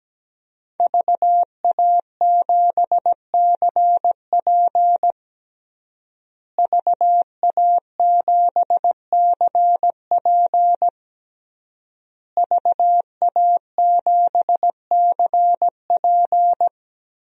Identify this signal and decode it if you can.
Morse code, heard as a keyed tone, about 17 wpm, 700 Hz. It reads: VA7CP VA7CP VA7CP
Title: DXCC-100 Call Signs - 5 Characters - Repeated 3x - 17wpm